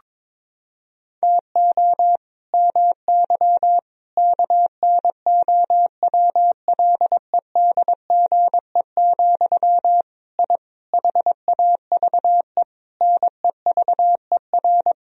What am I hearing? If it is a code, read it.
Morse code, heard as a keyed tone, about 22 wpm, 700 Hz. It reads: TO MY KNOWLEDGE, I HAVE NEVER
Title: Sentences from Top 1000 Words - Review Part 3 - 22wpm